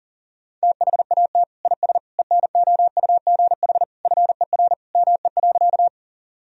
Morse code, heard as a keyed tone, about 40 wpm, 700 Hz. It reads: THAT IS ENOUGH FER ME.
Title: Sentences from Top 400 Words and Abbreviations 40wpm